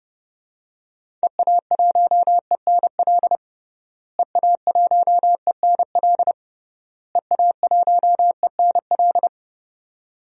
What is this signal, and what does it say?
Morse code, heard as a keyed tone, about 30 wpm, 700 Hz. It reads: EA1ENL EA1ENL EA1ENL
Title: DXCC-100 Call Signs - 6 Characters - Repeated 3x - 30wpm